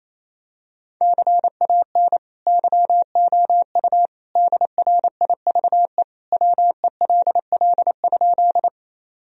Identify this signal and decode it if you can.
Morse code, heard as a keyed tone, about 28 wpm, 700 Hz. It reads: CAN YOU DRIVE WELL?
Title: Sentences from Top 1000 Words - Review Part 2 - 28wpm